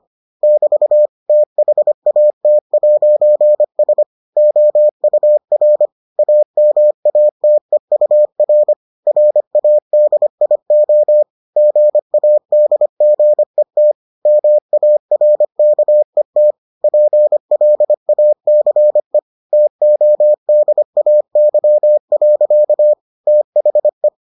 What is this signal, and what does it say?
Morse code, heard as a keyed tone, about 25 wpm, 600 Hz. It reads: <BT> THAT'S OUR AMATEUR RADIO GADGET MARKET PLACE TODAY. THE